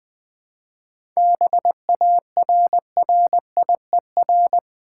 Morse code, heard as a keyed tone, about 20 wpm, 700 Hz. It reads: BARRIER
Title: 7 Letter Words - First 3 Letters Given - 20wpm